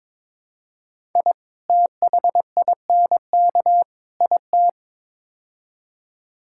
Morse code, heard as a keyed tone, about 22 wpm, 700 Hz. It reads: I THINK IT
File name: practice / 22wpm / Build A Phrase - 5-Word Phrases - 22wpm